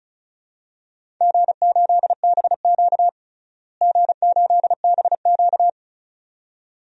Morse code, heard as a keyed tone, about 34 wpm, 700 Hz. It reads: G8BQ G8BQ
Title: DXCC-100 Call Signs - 4 Characters - Repeated 2x - 35wpm